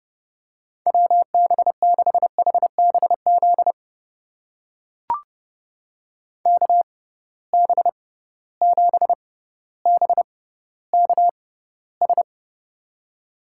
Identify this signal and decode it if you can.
Morse code, heard as a keyed tone, about 30 wpm, spaced as for 15 wpm, 700 Hz. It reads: WB6HBZ KB7BKS
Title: USA Call Signs - 6 Characters - 6x Character Spacing - 30wpm